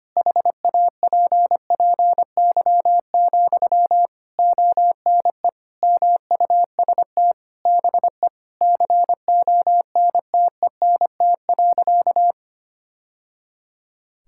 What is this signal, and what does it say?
Morse code, heard as a keyed tone, about 25 wpm, 700 Hz. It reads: HAPPY, ONE MUST BE CONTENT.
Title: Sentences from Top 1000 Words - Review Part 6 - 25wpm